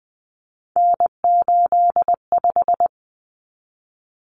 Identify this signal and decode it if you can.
Morse code, heard as a keyed tone, about 20 wpm, 700 Hz. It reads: N85
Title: Common Cut Numbers - 3-Digit 20wpm